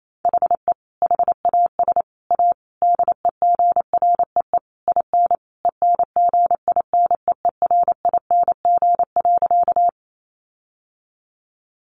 Morse code, heard as a keyed tone, about 28 wpm, 700 Hz. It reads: HE HAS A DEGREE IN ENGINEERING.
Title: Sentences from Top 1500 Words 28wpm